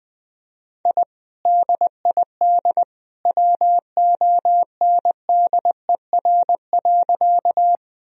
Morse code, heard as a keyed tone, about 20 wpm, 700 Hz. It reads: I DID WONDER.
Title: Sentences from Top 1000 Words - Review Part 3 - 20wpm